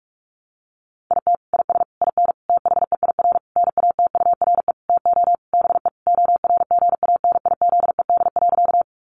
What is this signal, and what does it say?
Morse code, heard as a keyed tone, about 44 wpm, 700 Hz. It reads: IT IS IN THEIR NATURE TO BE ORGANIZED.
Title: Sentences from Top 1300 Words 45wpm